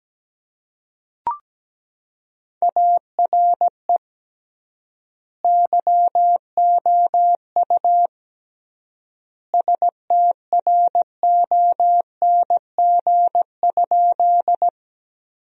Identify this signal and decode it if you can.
Morse code, heard as a keyed tone, about 17 wpm, 700 Hz. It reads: ARE YOU STRONG?
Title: 3x Word-Spacing - Sentences from Top 500 Words - 17wpm